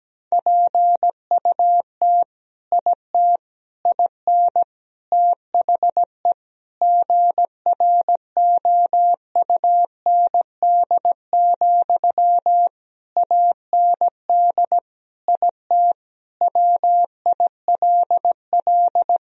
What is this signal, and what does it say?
Morse code, heard as a keyed tone, about 17 wpm, 700 Hz. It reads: PUT IT IN THE GROUND, AND IT WILL
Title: Sentences from Top 1000 Words - Review Part 2 - 17wpm